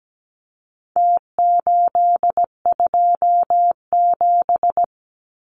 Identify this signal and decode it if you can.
Morse code, heard as a keyed tone, about 17 wpm, 700 Hz. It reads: T827
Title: Numbers - Contest Style - 4-Digit 17wpm